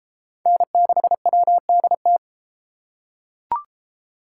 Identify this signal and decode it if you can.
Morse code, heard as a keyed tone, about 31 wpm, 700 Hz. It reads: N6WDT